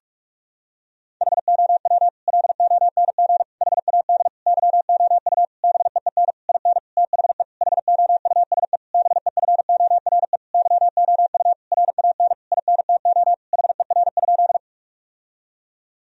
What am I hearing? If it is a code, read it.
Morse code, heard as a keyed tone, about 41 wpm, 700 Hz. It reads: HOW LONG HAD YOU BEEN IN THE HOUSE BEFORE YOU RAN INTO HER?
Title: Sentences from Top 1000 Words - Review Part 3 - 45wpm